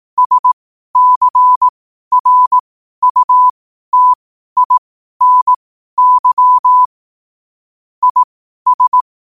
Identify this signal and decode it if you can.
Morse code, heard as a keyed tone, about 18 wpm, spaced as for 12 wpm, 1000 Hz. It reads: SCRUTINY IS
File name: qotdfiles / 12wpm